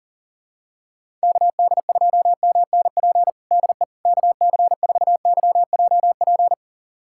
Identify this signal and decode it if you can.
Morse code, heard as a keyed tone, about 40 wpm, 700 Hz. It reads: KD2MNP DE KC4YJP
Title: USA Callsign DE Callsign 40wpm